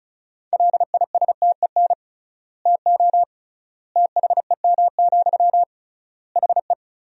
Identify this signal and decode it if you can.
Morse code, heard as a keyed tone, about 34 wpm, 700 Hz. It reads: LISTEN TO THEM, HE
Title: 3x Word-Spacing - Sentences from Top 400 Words - 35wpm